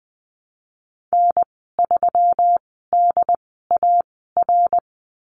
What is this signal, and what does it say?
Morse code, heard as a keyed tone, about 20 wpm, spaced as for 15 wpm, 700 Hz. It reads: N3DAR
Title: 186 Lesson 30g - Letter b - Call Signs 20wpm